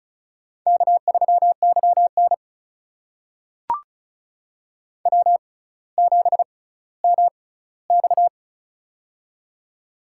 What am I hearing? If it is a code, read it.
Morse code, heard as a keyed tone, about 36 wpm, spaced as for 18 wpm, 700 Hz. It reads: K3YN W7MX
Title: USA Call Signs - 4 Characters - 6x Character Spacing - 35wpm